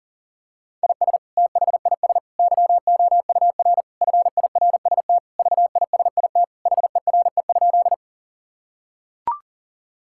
Morse code, heard as a keyed tone, about 40 wpm, 700 Hz. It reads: IS THIS YOUR FIRST VISIT HERE?
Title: Sentences from Top 1000 Words - Review Part 6 - 40wpm